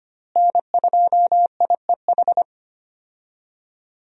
Morse code, heard as a keyed tone, about 25 wpm, 700 Hz. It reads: N2IEH
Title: USA Call Signs - 5 Characters - Repeated 3x - 25wpm